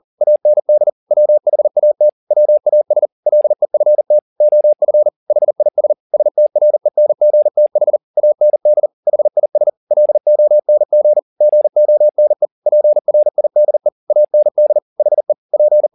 Morse code, heard as a keyed tone, about 40 wpm, 600 Hz. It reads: AND WHAT WAS LEFT OF HIS STRENGTH AND HIS LONG GONE PRIDE AND HE P
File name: the old man and the sea 2 40wpm